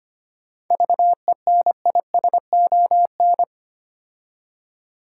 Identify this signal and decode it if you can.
Morse code, heard as a keyed tone, about 25 wpm, 700 Hz. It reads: VENISON
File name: practice / 25wpm / Difficult Words with Hints - 25wpm